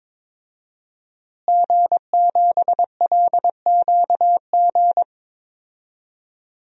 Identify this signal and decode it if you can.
Morse code, heard as a keyed tone, about 22 wpm, 700 Hz. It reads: G7LQG
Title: DXCC-100 Call Signs - 5 Characters - Repeated 2x - 22wpm